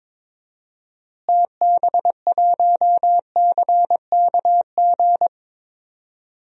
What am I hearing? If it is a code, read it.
Morse code, heard as a keyed tone, about 21 wpm, 700 Hz. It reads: TB1CKG
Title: DXCC-100 Call Signs - 6 Characters - Repeated 3x - 22wpm